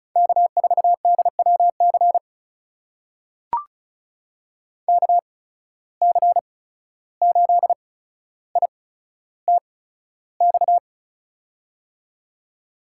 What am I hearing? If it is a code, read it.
Morse code, heard as a keyed tone, about 36 wpm, spaced as for 14 wpm, 700 Hz. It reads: K4DWC KC8ITX
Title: USA Call Signs - 8x Character Spacing - 35wpm